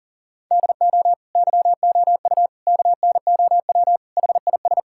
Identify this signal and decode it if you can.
Morse code, heard as a keyed tone, about 40 wpm, 700 Hz. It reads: DO YOU KNOW HIS
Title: Sentences from Top 1000 Words - Review Part 6 - 40wpm